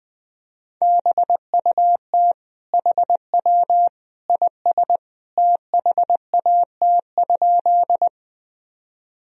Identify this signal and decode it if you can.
Morse code, heard as a keyed tone, about 20 wpm, 700 Hz. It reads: BUT HW IS THAT?
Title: Sentences from Top 100 Words and Abbreviations 20wpm